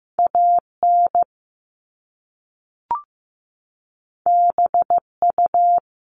Morse code, heard as a keyed tone, about 15 wpm, 700 Hz. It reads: AN BU
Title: Top Two Letter Combinations in Words 15wpm